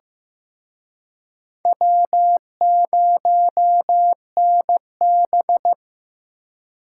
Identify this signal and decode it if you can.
Morse code, heard as a keyed tone, about 18 wpm, 700 Hz. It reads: W0NB
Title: Speed Racing - USA 1x2 Call Signs 15wpm